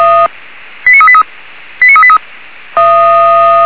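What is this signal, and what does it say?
Five-tone selective call (ZVEI), heard as quick runs of five tones, tones of about 70 ms. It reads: 89383 89383
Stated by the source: ZVEI five-tone selective call received on SDR.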